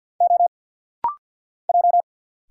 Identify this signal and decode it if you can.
Morse code, heard as a keyed tone, about 48 wpm, 700 Hz. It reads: O J
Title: Single Letter-Number - Rapid-Fire 50wpm